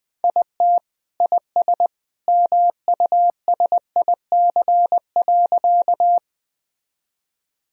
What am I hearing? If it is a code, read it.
Morse code, heard as a keyed tone, about 21 wpm, 700 Hz. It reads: IT IS MUSIC.
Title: Sentences from Top 1000 Words - Review Part 1 - 20wpm